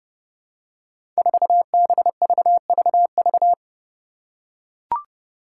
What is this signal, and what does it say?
Morse code, heard as a keyed tone, about 29 wpm, 700 Hz. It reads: 4BVVV
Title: Five Challenging Letter-Number Combinations 30wpm